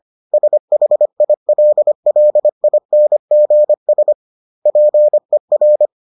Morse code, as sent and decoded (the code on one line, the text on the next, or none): ... .... .. .-.. .-.. .. -. --. ... / .--. .-.
SHILLINGS PER